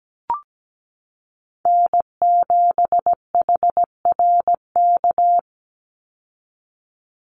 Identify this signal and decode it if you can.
Morse code, heard as a keyed tone, about 17 wpm, 700 Hz. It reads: N7HRK
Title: Call Signs - 17wpm